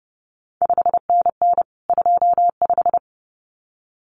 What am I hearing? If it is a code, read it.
Morse code, heard as a keyed tone, about 32 wpm, 700 Hz. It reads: 5NN 25
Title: CQ WW DX Contest Exchange - 30wpm